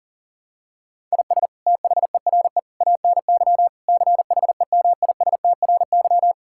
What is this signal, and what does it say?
Morse code, heard as a keyed tone, about 40 wpm, 700 Hz. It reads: IS THERE ANY CHEMISTRY
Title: Sentences from Top 1000 Words - Review Part 6 - 40wpm